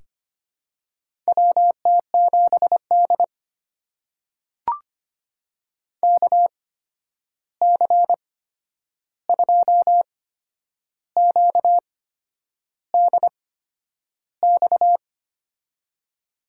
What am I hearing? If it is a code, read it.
Morse code, heard as a keyed tone, about 25 wpm, spaced as for 9 wpm, 700 Hz. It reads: WT7D KC2QDX
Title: USA Call Signs - 8x Character Spacing - 25wpm